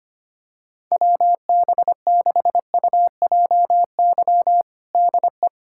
Morse code, heard as a keyed tone, about 25 wpm, 700 Hz. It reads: WB6UJY DE
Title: USA Callsign DE Callsign 25wpm